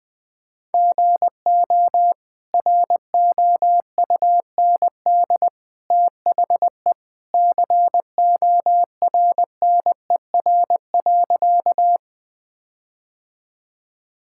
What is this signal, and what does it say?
Morse code, heard as a keyed tone, about 21 wpm, 700 Hz. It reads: GO ROUND THE CORNER.
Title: Sentences from Top 1600 Words 20wpm